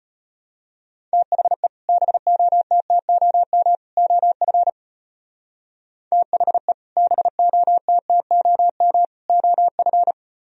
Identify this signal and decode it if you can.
Morse code, heard as a keyed tone, about 31 wpm, 700 Hz. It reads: THE BOTTOM OF THE BOTTOM OF
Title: Speed-Racing - Sets of 3 Words 25wpm